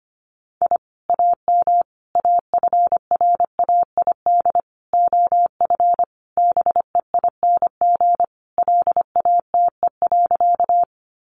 Morse code, heard as a keyed tone, about 25 wpm, 700 Hz. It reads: I AM AFRAID OF BEING LATE.
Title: Sentences from Top 1500 Words 25wpm